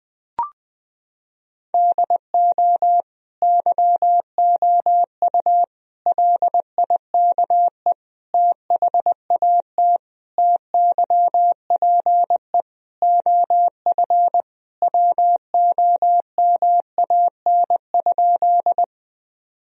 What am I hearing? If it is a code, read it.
Morse code, heard as a keyed tone, about 20 wpm, 700 Hz. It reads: DO YOU LIKE THAT TYPE OF WOMAN?
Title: Sentences from Top 1000 Words - Review Part 3 - 20wpm